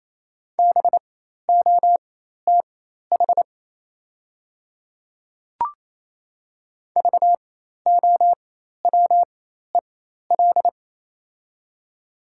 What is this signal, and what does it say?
Morse code, heard as a keyed tone, about 28 wpm, spaced as for 13 wpm, 700 Hz. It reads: BOTH VOWEL
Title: 4x Character Spacing - Top 500 Words - 28wpm